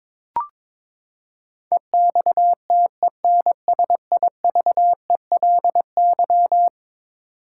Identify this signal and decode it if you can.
Morse code, heard as a keyed tone, about 22 wpm, 700 Hz. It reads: EXTENSIVELY
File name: practice / 22wpm / Eleven Letter Words - 22wpm